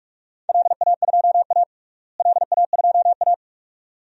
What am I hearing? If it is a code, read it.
Morse code, heard as a keyed tone, about 41 wpm, 700 Hz. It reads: LA2A LA2A
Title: DXCC-100 Call Signs - 4 Characters - Repeated 2x - 45wpm